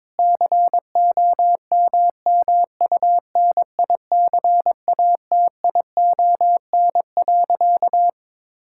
Morse code, heard as a keyed tone, about 22 wpm, 700 Hz. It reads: COMMUNICATION.
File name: Difficult Sentences with Hints - 22wpm